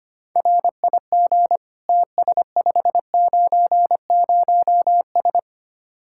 Morse code, heard as a keyed tone, about 31 wpm, 700 Hz. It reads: RIG TS590S
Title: Speed-Racing Common QSO Phrases Encore - 25wpm